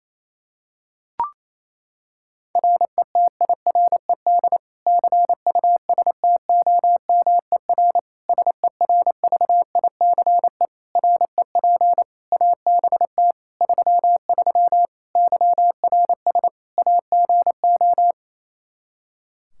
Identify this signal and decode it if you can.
Morse code, heard as a keyed tone, about 27 wpm, 700 Hz. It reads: RETIRED CUSTOMER SERVICE REP ABT 33 YRS AGO
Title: Practice QSO - Exchange 3 - More Info - 28wpm